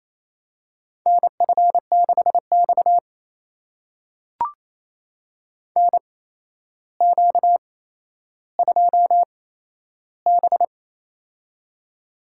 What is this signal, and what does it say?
Morse code, heard as a keyed tone, about 28 wpm, spaced as for 12 wpm, 700 Hz. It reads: NF6X NQ2B
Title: USA Call Signs - 4 Characters - 8x Character Spacing - 28wpm